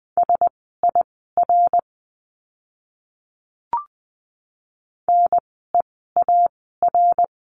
Morse code, heard as a keyed tone, about 20 wpm, spaced as for 14 wpm, 700 Hz. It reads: SIR NEAR
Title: 054 Lesson 10f - Letter r - Words 20wpm